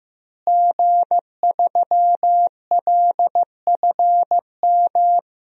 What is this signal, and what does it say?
Morse code, heard as a keyed tone, about 15 wpm, 700 Hz. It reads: G3LFM
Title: DXCC-100 Call Signs - 5 Characters - Repeated 3x - 15wpm